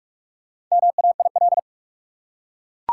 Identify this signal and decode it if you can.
Morse code, heard as a keyed tone, about 43 wpm, 700 Hz. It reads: MAIL
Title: Four Letter Words - 45wpm